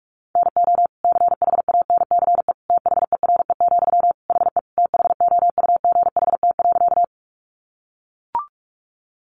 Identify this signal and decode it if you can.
Morse code, heard as a keyed tone, about 44 wpm, 700 Hz. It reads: NO CHANCE THERE, HE THOUGHT.
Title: Sentences from Top 800 Words 45wpm